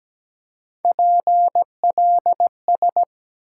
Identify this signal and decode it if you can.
Morse code, heard as a keyed tone, about 17 wpm, 700 Hz. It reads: PLS
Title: Top 90 QSO Elements 17wpm